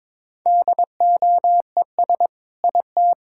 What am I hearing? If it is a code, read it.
Morse code, heard as a keyed tone, about 21 wpm, 700 Hz. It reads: DOES IT
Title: Sentences from Top 1000 Words - Review Part 6 - 22wpm